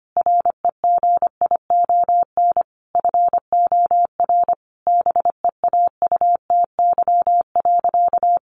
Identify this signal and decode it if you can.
Morse code, heard as a keyed tone, about 25 wpm, 700 Hz. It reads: REGION FOR BEAUTY.
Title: Sentences from Top 1500 Words 25wpm